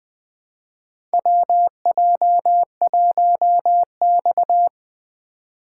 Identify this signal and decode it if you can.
Morse code, heard as a keyed tone, about 21 wpm, 700 Hz. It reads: WJ1X
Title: USA Call Signs - 4 Characters - 2x Character Spacing - 20wpm